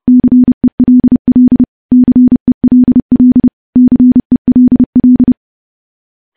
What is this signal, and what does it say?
Morse code, heard as a keyed tone, about 30 wpm, 250 Hz. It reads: CELL CELL CELL
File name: morse / morse-training-30WPM